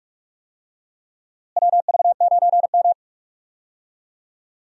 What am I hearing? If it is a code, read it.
Morse code, heard as a keyed tone, about 42 wpm, 700 Hz. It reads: WV9M
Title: USA Call Signs - 4 Characters - Repeated 2x - 45wpm